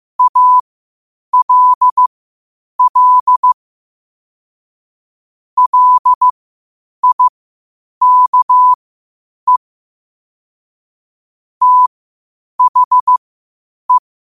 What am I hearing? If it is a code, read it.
Morse code, heard as a keyed tone, about 15 wpm, spaced as for 8 wpm, 1000 Hz. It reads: ALL LIKE THE
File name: qotdfiles / 8wpm